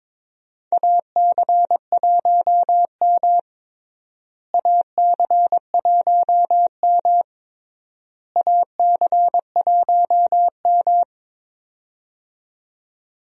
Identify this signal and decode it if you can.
Morse code, heard as a keyed tone, about 22 wpm, 700 Hz. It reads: AC1M AC1M AC1M
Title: USA Call Signs - 4 Characters - Repeated 3x - 22wpm